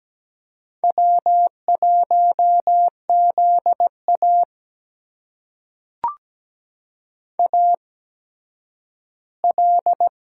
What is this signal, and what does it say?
Morse code, heard as a keyed tone, about 17 wpm, spaced as for 10 wpm, 700 Hz. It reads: W1ZA AL
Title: USA Call Signs - 4 Characters - 8x Character Spacing - 17wpm